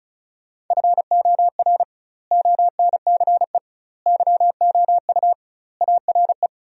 Morse code, heard as a keyed tone, about 36 wpm, 700 Hz. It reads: FOR ONCE YOU ARE
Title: 2x Word-Spacing - Sentences from Top 300 Words - 35wpm